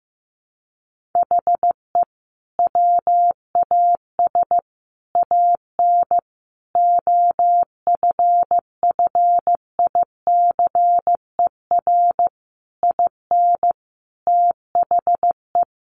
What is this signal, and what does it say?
Morse code, heard as a keyed tone, about 15 wpm, 700 Hz. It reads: HE WAS AN OFFICER IN THE
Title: Sentences from Top 1200 Words 15wpm